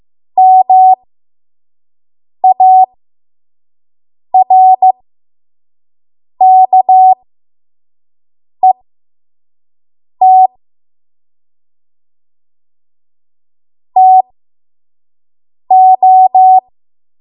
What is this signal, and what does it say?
Morse code, heard as a keyed tone, about 15 wpm, spaced as for 5 wpm, 750 Hz. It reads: MARKET TO
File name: Morse / 05WPM